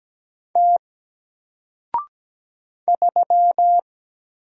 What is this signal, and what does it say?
Morse code, heard as a keyed tone, about 17 wpm, 700 Hz. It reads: T 3
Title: Single Letter-Number Alt 17wpm